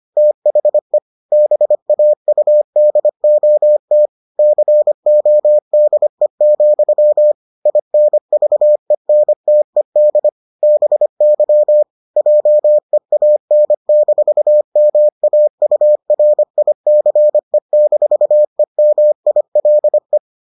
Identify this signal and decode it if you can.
Morse code, heard as a keyed tone, about 25 wpm, 600 Hz. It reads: THE BAUDOT CODE, INVENTED BY JEAN-MAURICE-EMILE